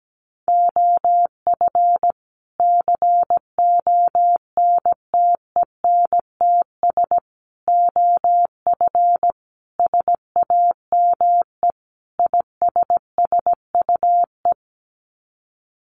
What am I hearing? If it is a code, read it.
Morse code, heard as a keyed tone, about 17 wpm, 700 Hz. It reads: OF CONTENTS OF SAME ISSUE
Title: Sets of 5 Words Encore2 17wpm